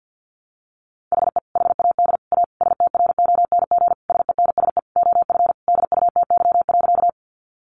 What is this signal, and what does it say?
Morse code, heard as a keyed tone, about 50 wpm, 700 Hz. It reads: HE HAD A STRONG SENSE OF DUTY.
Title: Sentences from Top 1500 Words 50wpm